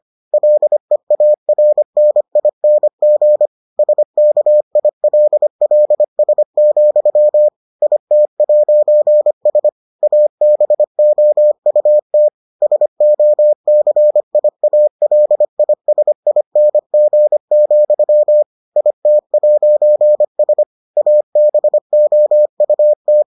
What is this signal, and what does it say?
Morse code, heard as a keyed tone, about 25 wpm, 600 Hz. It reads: LEARNING SKILLS, IT'S ABOUT SOCIALISING, IT'S ABOUT